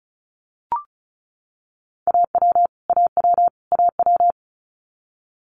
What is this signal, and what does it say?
Morse code, heard as a keyed tone, about 36 wpm, 700 Hz. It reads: AW AW AW
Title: Multi-letter Phonemes Repeated 3x 35wpm